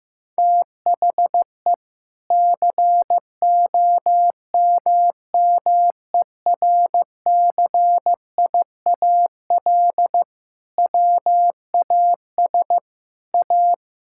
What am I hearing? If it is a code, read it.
Morse code, heard as a keyed tone, about 15 wpm, 700 Hz. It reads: THE COMMERCIAL WAS A
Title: Sentences from Top 1000 Words - Review Part 5 - 15wpm